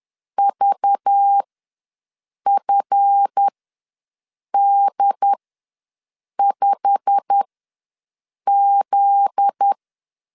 Morse code, heard as a keyed tone, about 10 wpm, spaced as for 6 wpm, 800 Hz. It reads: VFD5Z
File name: LEZIONI / 8 YJ45 / 8 YJ45 ws 10wpm 3843